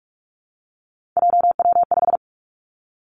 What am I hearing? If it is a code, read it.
Morse code, heard as a keyed tone, about 46 wpm, 700 Hz. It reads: JW5
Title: Three Challenging Letter-Number Combinations 45wpm